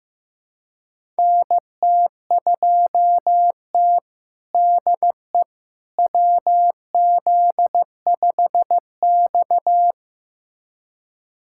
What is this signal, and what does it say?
Morse code, heard as a keyed tone, about 16 wpm, 700 Hz. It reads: NT2T DE WZ5X
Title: USA Callsign DE Callsign - 4 Characters - 15wpm